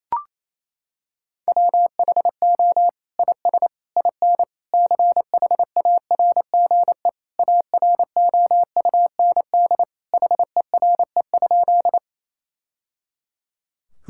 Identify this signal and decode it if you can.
Morse code, heard as a keyed tone, about 27 wpm, 700 Hz. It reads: WHO IS IN CHARGE AROUND HERE?
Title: Sentences from Top 1000 Words - Review Part 6 - 28wpm